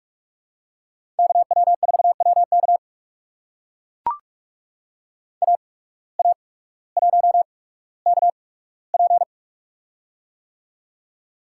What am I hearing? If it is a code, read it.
Morse code, heard as a keyed tone, about 44 wpm, spaced as for 19 wpm, 700 Hz. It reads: KW4WK AA1KP
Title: USA Call Signs - 5 Characters - 8x Character Spacing - 45wpm